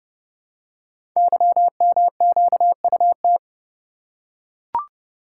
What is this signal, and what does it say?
Morse code, heard as a keyed tone, about 29 wpm, 700 Hz. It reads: YMQUT
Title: Code Groups - 5 Letters - 30wpm